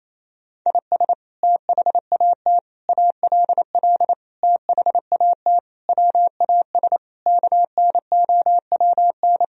Morse code, heard as a keyed tone, about 27 wpm, 700 Hz. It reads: IS THAT ALL THAT WAS KNOWN
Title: Sentences from Top 1000 Words - Review Part 3 - 28wpm